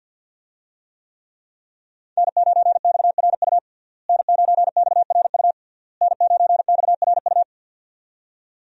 Morse code, heard as a keyed tone, about 48 wpm, 700 Hz. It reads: N9XRU N9XRU N9XRU
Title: USA Call Signs - 5 Characters - Repeated 3x - 50wpm